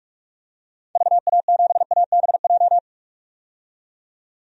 Morse code, heard as a keyed tone, about 45 wpm, 700 Hz. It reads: VA7ABJ
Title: DXCC-100 Call Signs - 6 Characters - 45wpm